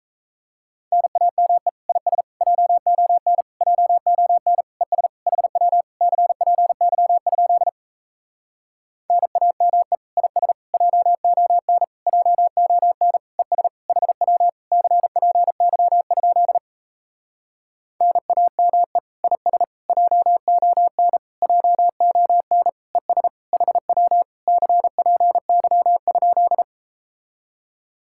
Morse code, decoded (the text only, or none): NAME IS JON JON ES HW CPY? NAME IS JON JON ES HW CPY? NAME IS JON JON ES HW CPY?